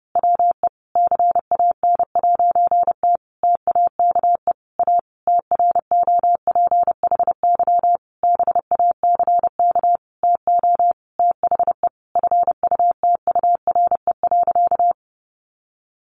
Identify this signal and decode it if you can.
Morse code, heard as a keyed tone, about 30 wpm, 700 Hz. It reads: WE CAN'T TAKE A TROPHY BACK TO THE FUTURE.
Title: A Sound Of Thunder 30wpm